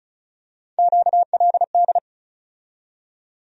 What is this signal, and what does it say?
Morse code, heard as a keyed tone, about 36 wpm, 700 Hz. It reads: QLD
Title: USA Call Sign Postfixes - 3 Letters 35wpm